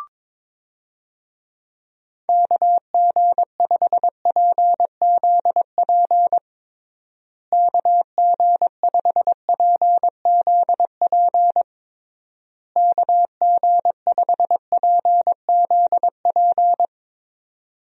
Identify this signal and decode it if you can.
Morse code, heard as a keyed tone, about 22 wpm, 700 Hz. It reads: KG5PZP KG5PZP KG5PZP
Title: USA Call Signs - 6 Characters - Repeated 3x - 22wpm